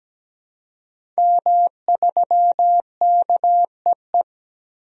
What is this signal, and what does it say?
Morse code, heard as a keyed tone, about 17 wpm, 700 Hz. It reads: M3KEE
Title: DXCC-100 Call Signs - 5 Characters - 17wpm